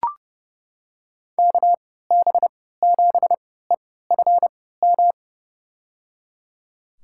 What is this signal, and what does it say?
Morse code, heard as a keyed tone, about 30 wpm, spaced as for 17 wpm, 700 Hz. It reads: KB7EFM
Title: USA Call Signs - 3x Character Spacing - 30wpm